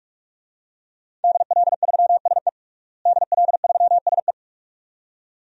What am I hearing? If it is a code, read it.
Morse code, heard as a keyed tone, about 46 wpm, 700 Hz. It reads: DL3SE DL3SE
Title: DXCC-100 Call Signs - 5 Characters - Repeated 2x - 45wpm